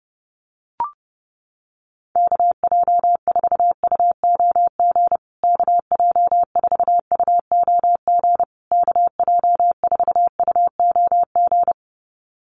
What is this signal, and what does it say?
Morse code, heard as a keyed tone, about 30 wpm, 700 Hz. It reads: KJ4UOG KJ4UOG KJ4UOG
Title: USA Call Signs Repeated 3x 30wpm